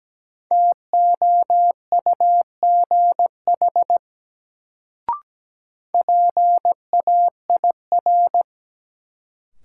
Morse code, heard as a keyed tone, about 17 wpm, 700 Hz. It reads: TOUGH PAIR
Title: Sentences from Top 1200 Words - New Word Drill - 17wpm